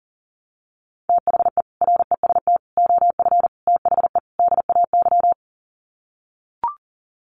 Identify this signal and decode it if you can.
Morse code, heard as a keyed tone, about 40 wpm, 700 Hz. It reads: THE REST OF THE DAY
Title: Sets of 5 Words 40wpm